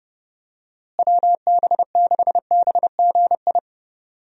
Morse code, heard as a keyed tone, about 32 wpm, 700 Hz. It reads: WB6BGI
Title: USA Call Signs - 6 Characters - Repeated 2x - 30wpm